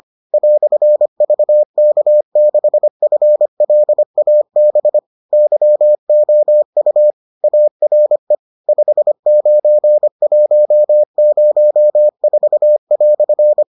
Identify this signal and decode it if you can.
Morse code, heard as a keyed tone, about 25 wpm, 600 Hz. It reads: "VK6FLAB YOU ARE 59104"